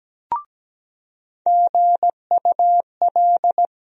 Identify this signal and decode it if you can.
Morse code, heard as a keyed tone, about 17 wpm, 700 Hz. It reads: GUL
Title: USA Call Sign Postfixes - 3 Letters 17wpm